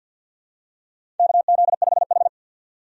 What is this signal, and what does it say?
Morse code, heard as a keyed tone, about 52 wpm, 700 Hz. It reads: KZ5H
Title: USA Call Signs - 4 Characters - 50wpm